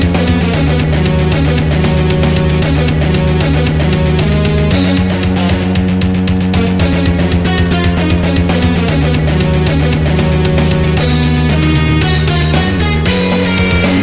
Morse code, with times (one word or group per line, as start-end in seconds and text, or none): none